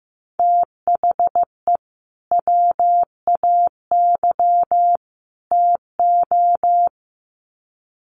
Morse code, heard as a keyed tone, about 15 wpm, 700 Hz. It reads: THE WAY TO
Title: Sets of 3 Words 15wpm